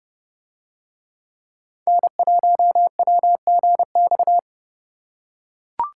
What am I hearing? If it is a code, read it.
Morse code, heard as a keyed tone, about 30 wpm, 700 Hz. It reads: N1WGX